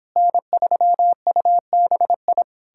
Morse code, heard as a keyed tone, about 31 wpm, 700 Hz. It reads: N3UBI